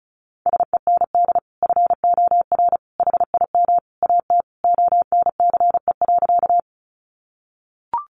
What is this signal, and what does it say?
Morse code, heard as a keyed tone, about 34 wpm, 700 Hz. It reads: SEND FOR HIM AT ONCE.
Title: Sentences from Top 1500 Words 35wpm